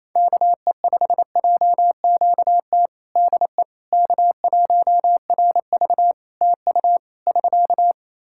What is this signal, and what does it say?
Morse code, heard as a keyed tone, about 27 wpm, 700 Hz. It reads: KE5JQT DE K1RV TU <SK>
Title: Practice QSO - Exchange 4 - Ending - 28wpm